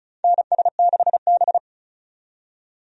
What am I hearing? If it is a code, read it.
Morse code, heard as a keyed tone, about 34 wpm, 700 Hz. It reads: NS6B
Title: USA Call Signs - 4 Characters - 35wpm